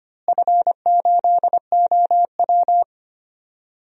Morse code, heard as a keyed tone, about 25 wpm, 700 Hz. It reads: F8OW
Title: DXCC-100 Call Signs - 4 Characters - Repeated 3x - 25wpm